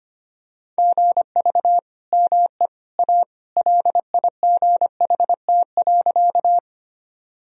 Morse code, heard as a keyed tone, about 25 wpm, 700 Hz. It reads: GV ME A LIGHT.
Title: Sentences from Top 200 Words and Abbreviations 25wpm